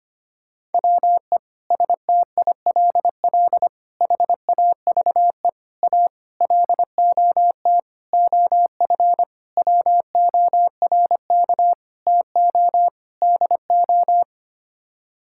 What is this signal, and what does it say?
Morse code, heard as a keyed tone, about 25 wpm, 700 Hz. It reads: WE STILL HAVE A LOT OF WORK TO DO
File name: Sets of 9 Words - 25wpm